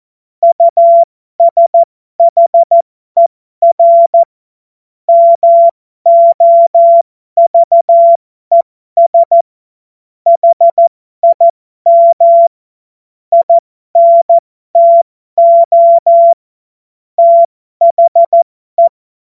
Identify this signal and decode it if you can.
Morse code, heard as a keyed tone, about 14 wpm, spaced as for 12 wpm, 680 Hz. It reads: USHER MOVES HIM INTO THE
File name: Morsekurs / Witze